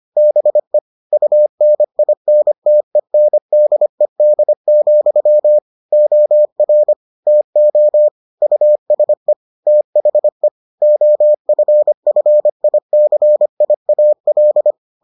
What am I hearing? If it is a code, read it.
Morse code, heard as a keyed tone, about 25 wpm, 600 Hz. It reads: BE UNINTENDED, OR TO USE THE OFFICIAL